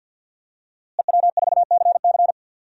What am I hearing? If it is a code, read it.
Morse code, heard as a keyed tone, about 52 wpm, 700 Hz. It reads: EW4CC
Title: DXCC-100 Call Signs - 5 Characters - Repeated 3x - 50wpm